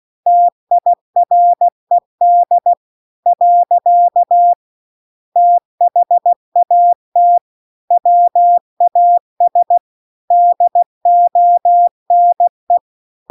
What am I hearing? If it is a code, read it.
Morse code, heard as a keyed tone, about 16 wpm, 700 Hz. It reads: TIRED . THAT WAS DONE